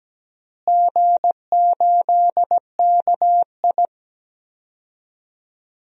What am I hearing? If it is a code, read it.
Morse code, heard as a keyed tone, about 17 wpm, 700 Hz. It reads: G8KI
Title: DXCC-100 Call Signs - 4 Characters - Repeated 2x - 17wpm